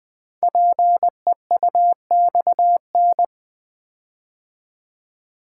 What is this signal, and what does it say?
Morse code, heard as a keyed tone, about 20 wpm, 700 Hz. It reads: PEUXN
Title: Code Groups - 5 Letters - 20wpm